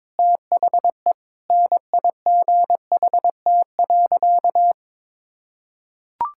Morse code, heard as a keyed tone, about 22 wpm, 700 Hz. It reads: THE NIGHT.
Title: Sentences from Top 1000 Words - Review Part 5 - 22wpm